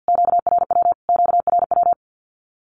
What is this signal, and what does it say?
Morse code, heard as a keyed tone, about 50 wpm, 700 Hz. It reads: QRW QRW
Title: Q-Codes Repeated 3x 50wpm